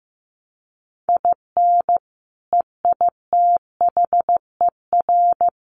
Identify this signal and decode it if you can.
Morse code, heard as a keyed tone, about 15 wpm, 700 Hz. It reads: IN EITHER
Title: Sentences from Top 1300 Words 15wpm